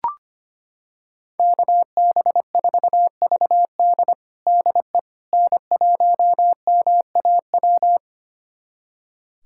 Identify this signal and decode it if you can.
Morse code, heard as a keyed tone, about 25 wpm, 700 Hz. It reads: KB4VD DE N1MAW
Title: USA Callsign DE Callsign - 5 Characters - 25wpm